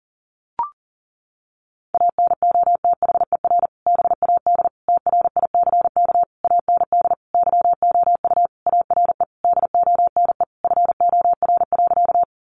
Morse code, heard as a keyed tone, about 37 wpm, 700 Hz. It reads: ANOTHER BAD TRICK AND YOU ARE DONE FOR.
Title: Sentences from Top 1500 Words 40wpm